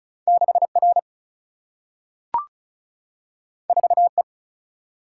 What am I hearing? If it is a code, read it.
Morse code, heard as a keyed tone, about 36 wpm, 700 Hz. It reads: 6R 4E
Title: Single-Letter Single-Number 35wpm